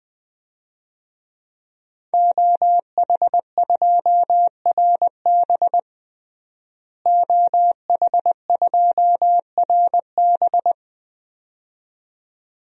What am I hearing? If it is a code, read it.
Morse code, heard as a keyed tone, about 21 wpm, 700 Hz. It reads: OH2RB OH2RB
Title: DXCC-100 Call Signs - 5 Characters - Repeated 2x - 20wpm